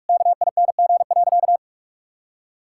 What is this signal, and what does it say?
Morse code, heard as a keyed tone, about 41 wpm, 700 Hz. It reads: KING.
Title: Sentences from Top 1000 Words - Review Part 6 - 45wpm